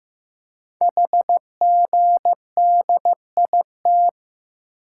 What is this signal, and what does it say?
Morse code, heard as a keyed tone, about 15 wpm, 700 Hz. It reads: HGDIT
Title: Code Groups - 5 Letters - 15wpm